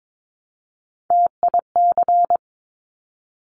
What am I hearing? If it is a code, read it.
Morse code, heard as a keyed tone, about 22 wpm, 700 Hz. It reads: TIC
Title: Top Three Letter Combinations in Words 22wpm